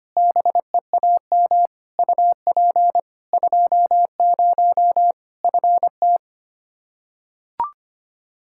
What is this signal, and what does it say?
Morse code, heard as a keyed tone, about 25 wpm, 700 Hz. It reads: BEAM UP 20 FT
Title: Practice QSO - Exchange 2 - Info - 25wpm